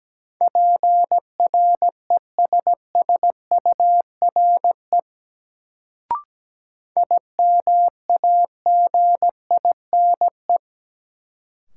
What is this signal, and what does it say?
Morse code, heard as a keyed tone, about 17 wpm, 700 Hz. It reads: PRESSURE IMAGINE
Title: Sentences from Top 1400 Words - New Word Drill Alt - 17wpm